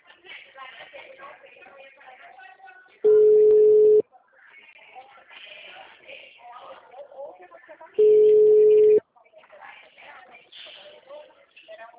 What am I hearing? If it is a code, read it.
Ringback tone (Europe, 425 Hz), with speech.
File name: out- -2009- - - .14937